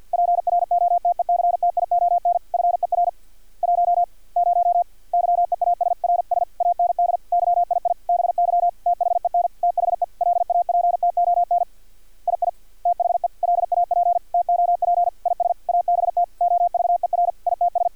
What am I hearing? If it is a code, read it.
Morse code, heard as a keyed tone, about 50 wpm, 700 Hz. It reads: PROTECTION FER 1 0 YEASRS AND YII BY THEN THE LAPTON II THE LAP TOP IS ABT OVER ITS
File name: archived / CRSnet